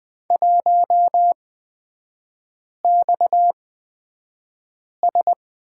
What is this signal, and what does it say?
Morse code, heard as a keyed tone, about 20 wpm, spaced as for 5 wpm, 700 Hz. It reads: 1XS
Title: 243 Lesson 39e - Letter z - Three Characters 20wpm